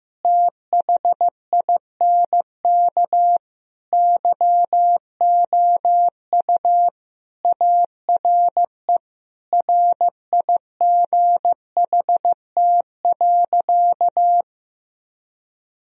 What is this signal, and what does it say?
Morse code, heard as a keyed tone, about 15 wpm, 700 Hz. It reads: THINK YOU ARE RIGHT.
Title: Sentences from Top 1000 Words - Review Part 4 - 15wpm